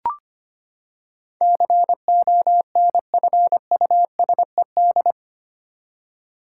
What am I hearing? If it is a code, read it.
Morse code, heard as a keyed tone, about 25 wpm, 700 Hz. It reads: CONFUSED
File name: Eight Letter Words - 25wpm